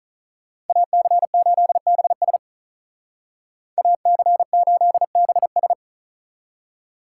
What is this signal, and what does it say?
Morse code, heard as a keyed tone, about 45 wpm, 700 Hz. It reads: AC8BS AC8BS